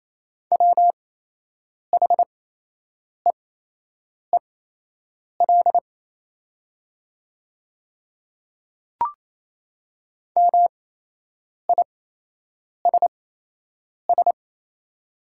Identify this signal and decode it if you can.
Morse code, heard as a keyed tone, about 27 wpm, spaced as for 8 wpm, 700 Hz. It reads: WHEEL MISS
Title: 8x Character Spacing - Top 500 Words - 28wpm